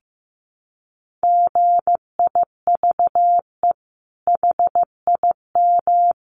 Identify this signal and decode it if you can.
Morse code, heard as a keyed tone, about 15 wpm, 700 Hz. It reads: GIVE HIM
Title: Sentences from Top 1200 Words 15wpm